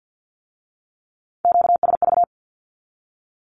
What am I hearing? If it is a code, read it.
Morse code, heard as a keyed tone, about 50 wpm, 700 Hz. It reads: QSV
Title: Q-Codes with Spoken Meaning - 50wpm